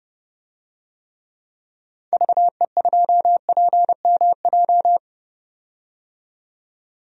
Morse code, heard as a keyed tone, about 29 wpm, 690 Hz. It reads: VE2PMJ
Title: DXCC-100 Call Signs - 6 Characters - 30wpm